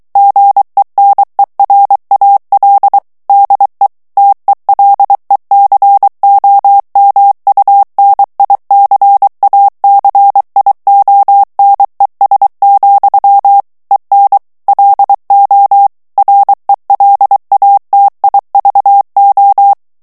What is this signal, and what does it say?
Morse code, heard as a keyed tone, about 23 wpm, 800 Hz. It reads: GENERAL DE TELECOMUNICACIONES, EN LO RELATIVO
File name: morse mp3 / ESPAGNOL TEXTE EN CLAIR 2 VITESSE 22 MOTS MINUTES